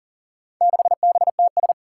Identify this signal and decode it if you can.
Morse code, heard as a keyed tone, about 37 wpm, 700 Hz. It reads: 6BTS
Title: Four Challenging Letter-Number Combinations 40wpm